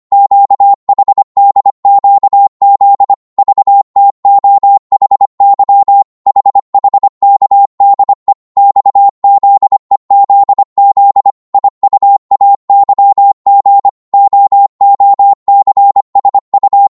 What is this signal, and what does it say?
Morse code, heard as a keyed tone, about 25 wpm, 800 Hz. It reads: QHDQZ VTOHY HHKDE XZEZZ IUAYG OOCSU